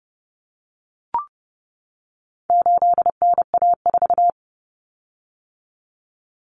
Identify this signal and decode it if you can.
Morse code, heard as a keyed tone, about 30 wpm, 700 Hz. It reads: 8NA4
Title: Common Cut Numbers - 4-Digit 30wpm